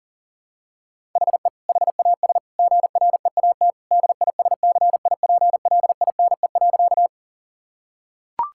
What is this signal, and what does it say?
Morse code, heard as a keyed tone, about 37 wpm, 700 Hz. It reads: HE HAS GREAT DISCIPLINE.
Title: Sentences from Top 1000 Words - Review Part 6 - 40wpm